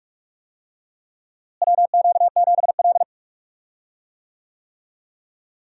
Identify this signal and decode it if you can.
Morse code, heard as a keyed tone, about 41 wpm, 700 Hz. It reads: WQ7L
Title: USA Call Signs - 4 Characters - Repeated 2x - 45wpm